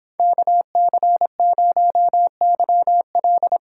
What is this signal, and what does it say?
Morse code, heard as a keyed tone, about 29 wpm, 700 Hz. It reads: KC0YL